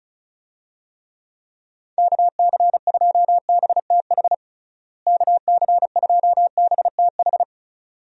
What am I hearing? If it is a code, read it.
Morse code, heard as a keyed tone, about 36 wpm, 700 Hz. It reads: KC2BTH KC2BTH
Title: USA Call Signs - 6 Characters - Repeated 2x - 35wpm